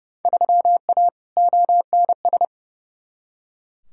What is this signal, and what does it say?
Morse code, heard as a keyed tone, about 30 wpm, 700 Hz. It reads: 3A ONS
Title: Field Day Exchanges - 30wpm